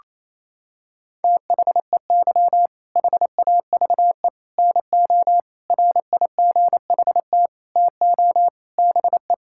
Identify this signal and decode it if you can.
Morse code, heard as a keyed tone, about 27 wpm, 700 Hz. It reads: THEY HAVE NO RIGHT TO BE
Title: Sentences from Top 1000 Words - Review Part 6 - 28wpm